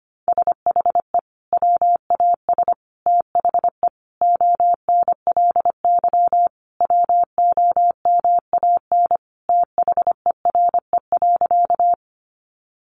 Morse code, heard as a keyed tone, about 25 wpm, 700 Hz. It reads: SHE WAS THE ONLY WOMAN THERE.
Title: Sentences from Top 1100 Words 25wpm